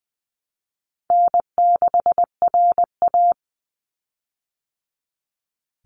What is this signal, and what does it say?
Morse code, heard as a keyed tone, about 20 wpm, 700 Hz. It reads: N6RA
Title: Easy Call Signs - 20wpm